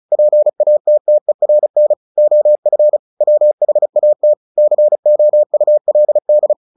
Morse code, heard as a keyed tone, about 34 wpm, 600 Hz. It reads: PATTERN OF WHAT COULD
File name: the old man and the sea 2 35wpm